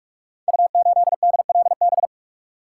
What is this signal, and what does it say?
Morse code, heard as a keyed tone, about 44 wpm, 700 Hz. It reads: U8DLB
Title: Code Groups - 5 Letter:Numbers - 45wpm